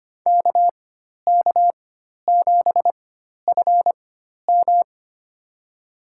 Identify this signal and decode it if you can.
Morse code, heard as a keyed tone, about 25 wpm, spaced as for 12 wpm, 700 Hz. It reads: KK7FM
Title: USA Call Signs - 4x Character Spacing - 25wpm